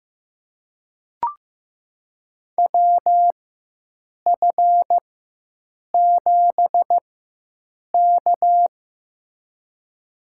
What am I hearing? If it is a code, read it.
Morse code, heard as a keyed tone, about 15 wpm, spaced as for 7 wpm, 700 Hz. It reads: WF7K
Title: USA Call Signs - 4 Characters - 4x Character Spacing - 15wpm